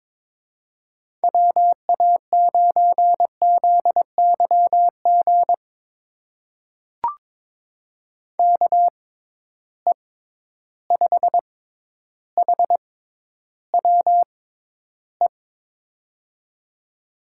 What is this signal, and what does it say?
Morse code, heard as a keyed tone, about 22 wpm, spaced as for 11 wpm, 700 Hz. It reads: WA9ZYG KE5HWE
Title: USA Call Signs - 6 Characters - 6x Character Spacing - 22wpm